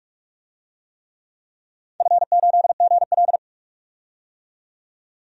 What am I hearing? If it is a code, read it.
Morse code, heard as a keyed tone, about 42 wpm, 700 Hz. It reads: F8GL
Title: DXCC-100 Call Signs - 4 Characters - 45wpm